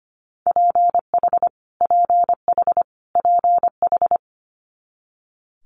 Morse code, heard as a keyed tone, about 25 wpm, 700 Hz. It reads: PH PH PH
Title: Multi-letter Phonemes Repeated 3x 25wpm